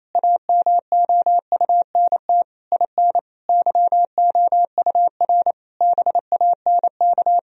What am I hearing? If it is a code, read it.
Morse code, heard as a keyed tone, about 28 wpm, 700 Hz. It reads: AMOUNT IN YOUR BANK
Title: Sentences from Top 1000 Words - Review Part 4 - 28wpm